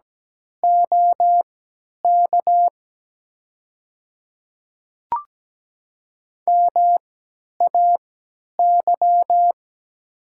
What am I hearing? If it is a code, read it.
Morse code, heard as a keyed tone, about 17 wpm, spaced as for 10 wpm, 700 Hz. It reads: OK MAY
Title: 3x Character Spacing - Top 300 Words - 17wpm